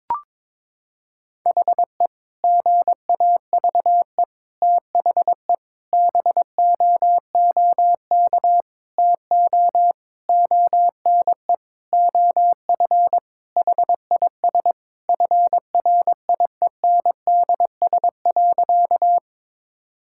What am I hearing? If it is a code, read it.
Morse code, heard as a keyed tone, about 22 wpm, 700 Hz. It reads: HE GAVE THE BOOK TO ONE OF HIS FRIENDS.
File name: practice / 22wpm / Sentences with 4-Word Phrases - 22wpm